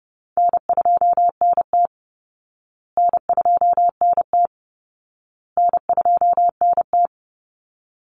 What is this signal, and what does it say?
Morse code, heard as a keyed tone, about 30 wpm, 700 Hz. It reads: N2NT N2NT N2NT
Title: Call Signs Repeated 2x - 30wpm